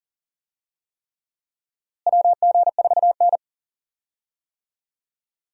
Morse code, heard as a keyed tone, about 40 wpm, 700 Hz. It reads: WG4N
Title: USA Call Signs - 4 Characters - Repeated 3x - 40wpm